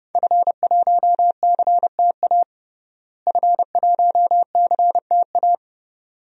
Morse code, heard as a keyed tone, about 30 wpm, 700 Hz. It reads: F1CTA F1CTA
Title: DXCC-100 Call Signs - 5 Characters - Repeated 2x - 30wpm